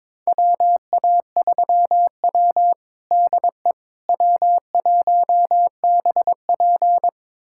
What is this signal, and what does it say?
Morse code, heard as a keyed tone, about 22 wpm, 700 Hz. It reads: WA3W DE W1BP
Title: USA Callsign DE Callsign - 4 Characters - 22wpm